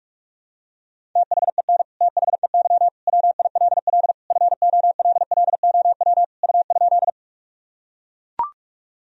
Morse code, heard as a keyed tone, about 41 wpm, 700 Hz. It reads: THEN THEY WILL FOLLOW U?
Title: Sentences from Top 200 Words and Abbreviations 45wpm